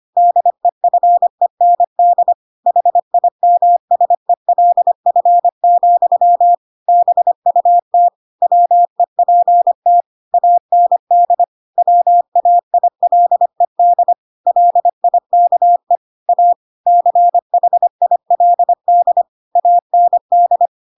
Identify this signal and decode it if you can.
Morse code, heard as a keyed tone, about 25 wpm, 700 Hz. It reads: DEFEND HIMSELF, BUT WEPT AND WAILED LIKE A CHILD AND
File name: Pinocchio / Pinocchio ch3 25 25 ik1hge